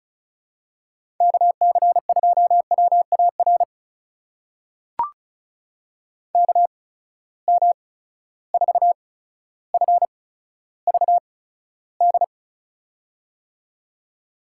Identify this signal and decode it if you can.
Morse code, heard as a keyed tone, about 36 wpm, spaced as for 15 wpm, 700 Hz. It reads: KC2WAR KM4FVD
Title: USA Call Signs - 6 Characters - 8x Character Spacing - 35wpm